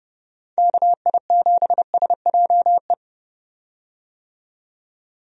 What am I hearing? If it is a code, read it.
Morse code, heard as a keyed tone, about 30 wpm, 700 Hz. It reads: KI7SJE
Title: USA Call Signs - 6 Characters - 30wpm